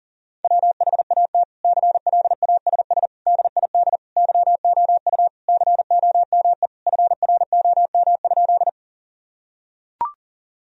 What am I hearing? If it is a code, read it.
Morse code, heard as a keyed tone, about 37 wpm, 700 Hz. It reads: WHAT CLASS DID YOU COME FROM?
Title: Sentences from Top 1000 Words - Review Part 2 - 40wpm